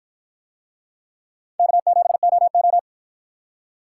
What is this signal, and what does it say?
Morse code, heard as a keyed tone, about 53 wpm, 700 Hz. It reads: K7OO